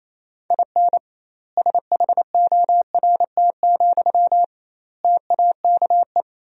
Morse code, heard as a keyed tone, about 28 wpm, 700 Hz. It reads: IN SHORT, TAKE
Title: 2x Word-Spacing - Sentences from Top 400 Words - 28wpm